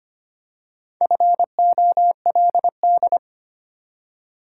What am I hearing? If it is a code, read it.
Morse code, heard as a keyed tone, about 25 wpm, 700 Hz. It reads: FOLD
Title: Four Letter Words - 25wpm